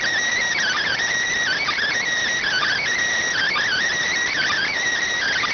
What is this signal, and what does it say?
Emergency Alert System (SAME) tones, heard as harsh buzzing bursts. It is heard from 0.6 to 4.7 s.